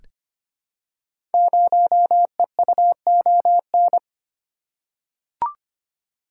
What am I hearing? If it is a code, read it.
Morse code, heard as a keyed tone, about 25 wpm, 700 Hz. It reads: 0EUON
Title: Code Groups - 5 Letter:Numbers - 25wpm